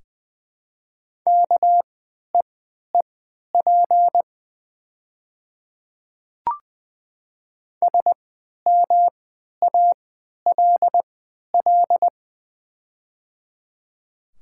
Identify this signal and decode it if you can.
Morse code, heard as a keyed tone, about 21 wpm, spaced as for 11 wpm, 700 Hz. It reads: KEEP SMALL
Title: 3x Character Spacing - Top 300 Words - 20wpm